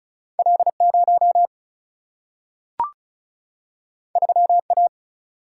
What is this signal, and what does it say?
Morse code, heard as a keyed tone, about 36 wpm, 700 Hz. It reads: L0 3A
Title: Single-Letter Single-Number 35wpm